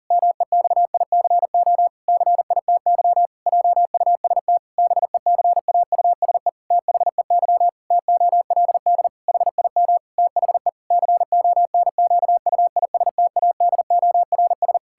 Morse code, heard as a keyed tone, about 40 wpm, 700 Hz. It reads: MEXICO CITY JUST BECAUSE THEY TOLD HIM THE CONQUISTADORS
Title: Pangram Challenge v2 - 40wpm